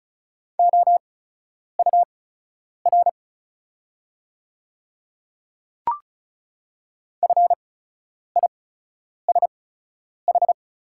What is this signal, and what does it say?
Morse code, heard as a keyed tone, about 36 wpm, spaced as for 10 wpm, 700 Hz. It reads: OUR FISH
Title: 8x Character Spacing - Top 500 Words - 35wpm